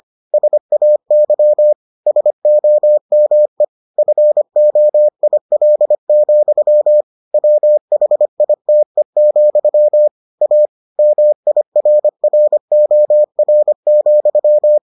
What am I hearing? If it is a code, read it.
Morse code, heard as a keyed tone, about 25 wpm, 600 Hz. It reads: SAY SOME FOIL, WHITE, A MIRROR,